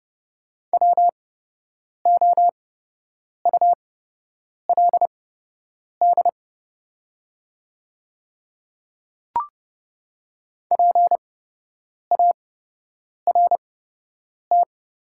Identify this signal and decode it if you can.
Morse code, heard as a keyed tone, about 30 wpm, spaced as for 8 wpm, 700 Hz. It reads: WOULD PART
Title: 8x Character Spacing - Top 300 Words - 30wpm